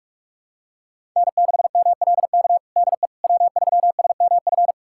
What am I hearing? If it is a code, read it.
Morse code, heard as a keyed tone, about 46 wpm, 700 Hz. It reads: N6MLK DE W3SMF
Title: USA Callsign DE Callsign - 5 Characters - 45wpm